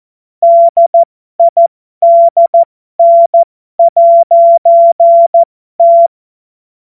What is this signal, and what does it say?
Morse code, heard as a keyed tone, about 14 wpm, spaced as for 12 wpm, 680 Hz. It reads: DIDN'T
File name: Morsekurs / Witze